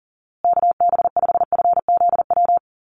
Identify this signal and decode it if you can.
Morse code, heard as a keyed tone, about 38 wpm, 700 Hz. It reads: KB5FZW
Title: USA Call Signs Repeated 3x 40wpm